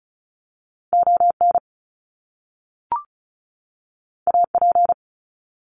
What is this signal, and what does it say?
Morse code, heard as a keyed tone, about 36 wpm, 700 Hz. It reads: ON AP
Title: Top Two Letter Combinations in Words 35wpm